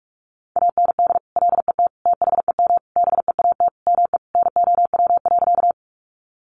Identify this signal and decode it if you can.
Morse code, heard as a keyed tone, about 46 wpm, 700 Hz. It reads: AND LET THEM BEAT ME NOW.
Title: Sentences from Top 1300 Words 45wpm